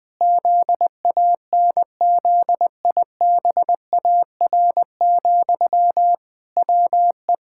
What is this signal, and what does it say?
Morse code, heard as a keyed tone, about 20 wpm, 700 Hz. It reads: ZANZIBAR, WE
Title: Pangram Challenge v2 - 20wpm